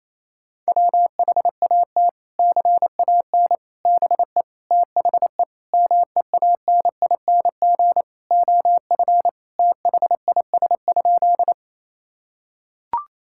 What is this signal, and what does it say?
Morse code, heard as a keyed tone, about 28 wpm, 700 Hz. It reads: WHAT CAN BE THE MEANING OF THIS?
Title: Sentences from Top 1000 Words - Review Part 6 - 28wpm